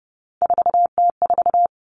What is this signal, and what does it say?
Morse code, heard as a keyed tone, about 30 wpm, 700 Hz. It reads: 4T4
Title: Three Challenging Letter-Number Combinations 30wpm